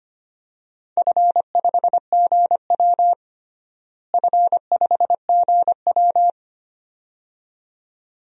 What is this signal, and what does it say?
Morse code, heard as a keyed tone, about 25 wpm, 700 Hz. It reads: F5GW F5GW
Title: DXCC-100 Call Signs - 4 Characters - Repeated 2x - 25wpm